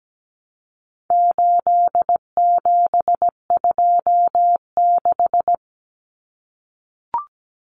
Morse code, heard as a keyed tone, about 17 wpm, 700 Hz. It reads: 8726
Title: Common Cut Numbers - 4-Digit 17wpm